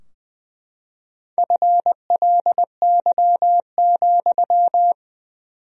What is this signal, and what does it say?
Morse code, heard as a keyed tone, about 20 wpm, 700 Hz. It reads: FLY,
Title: 4x Word-Spacing - Sentences from Top 500 Words - 20wpm